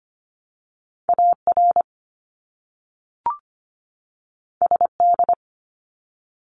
Morse code, heard as a keyed tone, about 25 wpm, 700 Hz. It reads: AR SD
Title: ARRL Sections 25wpm